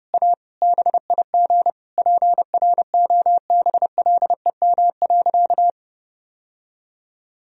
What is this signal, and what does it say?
Morse code, heard as a keyed tone, about 32 wpm, 700 Hz. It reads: A BIG PROBLEM.
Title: Sentences from Top 1000 Words - Review Part 4 - 30wpm